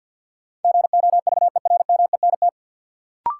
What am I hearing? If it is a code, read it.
Morse code, heard as a keyed tone, about 52 wpm, 700 Hz. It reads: GOVERMENT
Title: Sentences from Top 1300 Words - New Word Drill Alt - 50wpm